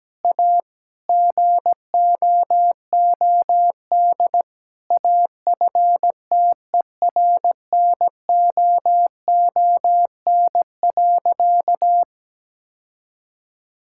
Morse code, heard as a keyed tone, about 17 wpm, 700 Hz. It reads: A GOOD AFTERNOON.
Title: Sentences from Top 1600 Words 17wpm